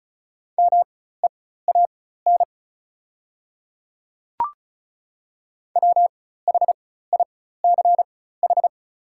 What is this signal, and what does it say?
Morse code, heard as a keyed tone, about 36 wpm, spaced as for 16 wpm, 700 Hz. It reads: MEAN WHICH
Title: 4x Character Spacing - Top 500 Words - 35wpm